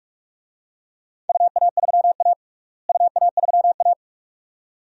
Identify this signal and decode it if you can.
Morse code, heard as a keyed tone, about 44 wpm, 700 Hz. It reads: UA3A UA3A
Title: DXCC-100 Call Signs - 4 Characters - Repeated 2x - 45wpm